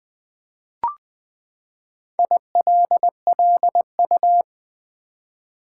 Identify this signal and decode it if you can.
Morse code, heard as a keyed tone, about 21 wpm, 700 Hz. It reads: ILLU
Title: Build A Word - 4 Letters At A Time - 8-Letter Words - 20wpm